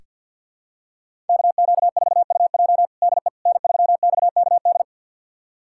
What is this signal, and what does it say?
Morse code, heard as a keyed tone, about 46 wpm, 700 Hz. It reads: KQ4UJ DE N3XKD
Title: USA Callsign DE Callsign - 5 Characters - 50wpm